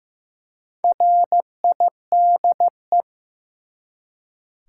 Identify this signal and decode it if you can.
Morse code, heard as a keyed tone, about 15 wpm, 700 Hz. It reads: RIDE
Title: Four Letter Words - 15wpm